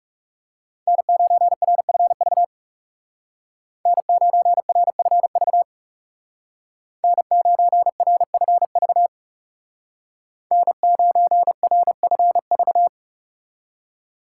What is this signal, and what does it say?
Morse code, heard as a keyed tone, about 37 wpm, 700 Hz. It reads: N9RFV N9RFV N9RFV N9RFV
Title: Speed Racing - USA 1x3 Call Signs 30wpm